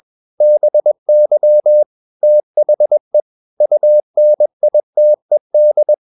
Morse code, heard as a keyed tone, about 21 wpm, 600 Hz. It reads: BY THE UNITED
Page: CW med Gnister